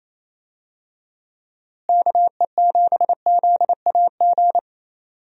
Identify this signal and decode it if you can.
Morse code, heard as a keyed tone, about 27 wpm, 700 Hz. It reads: KE7ZAG
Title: USA Call Signs - 6 Characters - Repeated 3x - 28wpm